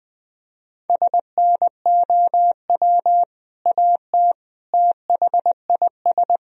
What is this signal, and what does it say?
Morse code, heard as a keyed tone, about 20 wpm, 700 Hz. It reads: SNOW AT THIS
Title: Sentences from Top 1000 Words - Review Part 3 - 20wpm